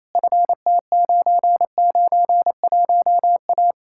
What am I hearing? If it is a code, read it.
Morse code, heard as a keyed tone, about 37 wpm, 700 Hz. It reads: FT991A